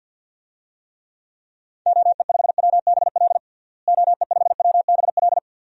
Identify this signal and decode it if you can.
Morse code, heard as a keyed tone, about 52 wpm, 700 Hz. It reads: OE5WBL OE5WBL
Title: DXCC-100 Call Signs - 6 Characters - Repeated 2x - 50wpm